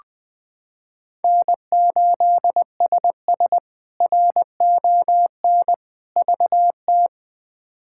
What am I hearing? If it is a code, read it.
Morse code, heard as a keyed tone, about 20 wpm, 700 Hz. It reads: N8SS RON VT
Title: K1USN SST QSO Practice - 20wpm